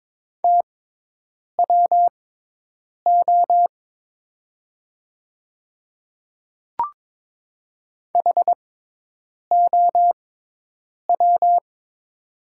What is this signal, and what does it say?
Morse code, heard as a keyed tone, about 22 wpm, spaced as for 8 wpm, 700 Hz. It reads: TWO HOW
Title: 6x Character Spacing - Top 100 Words - 22wpm